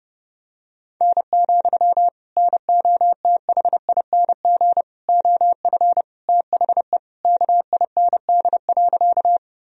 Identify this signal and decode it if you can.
Morse code, heard as a keyed tone, about 32 wpm, 700 Hz. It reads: N, NOTHING OF THE KIND.
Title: Sentences from Top 500 Words and Abbreviations 30wpm